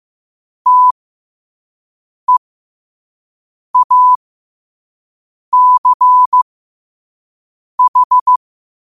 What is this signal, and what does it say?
Morse code, heard as a keyed tone, about 15 wpm, spaced as for 5 wpm, 1000 Hz. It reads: TEACH
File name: qotdfiles / 5wpm